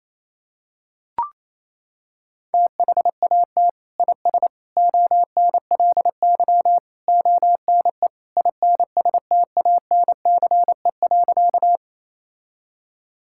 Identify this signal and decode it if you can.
Morse code, heard as a keyed tone, about 28 wpm, 700 Hz. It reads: THAT IS ONLY ONE INSTANCE.
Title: Sentences from Top 1000 Words - Review Part 5 - 28wpm